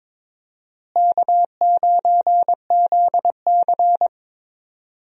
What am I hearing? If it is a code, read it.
Morse code, heard as a keyed tone, about 22 wpm, 700 Hz. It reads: K9ZC
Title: USA Call Signs - 4 Characters - Repeated 3x - 22wpm